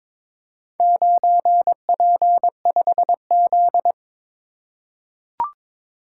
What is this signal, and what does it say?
Morse code, heard as a keyed tone, about 22 wpm, 700 Hz. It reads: 9P5Z
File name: practice / 22wpm / Four Letter-Number Combinations - 22wpm